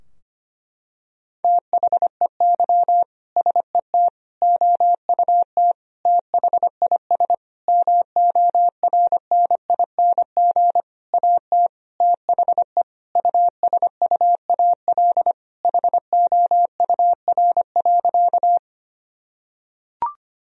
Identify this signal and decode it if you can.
Morse code, heard as a keyed tone, about 25 wpm, 700 Hz. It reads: THEY SET OUT THIS MORNING AT THE USUAL HOUR.
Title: Sentences from Top 1000 Words - Review Part 2 - 25wpm